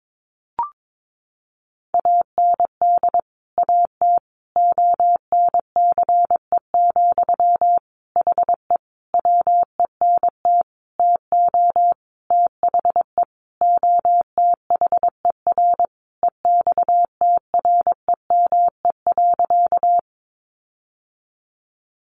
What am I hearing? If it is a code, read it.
Morse code, heard as a keyed tone, about 22 wpm, 700 Hz. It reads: AND AT ONCE, HE WENT TO THE OTHER EXTREME.
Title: Sentences from Top 1300 Words 22wpm